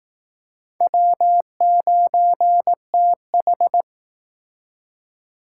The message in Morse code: W9TH

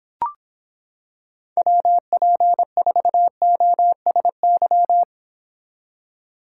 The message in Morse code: WP4OSY

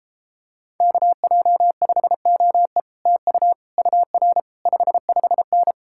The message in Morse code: KJ5OE TU UR 55N

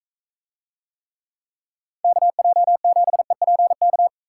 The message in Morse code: KJ7EPK

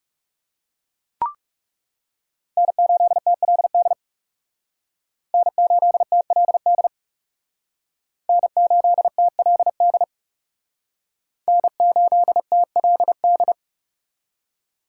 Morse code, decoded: N8TLD N8TLD N8TLD N8TLD

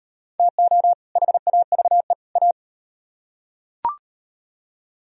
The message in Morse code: TO HAVE A

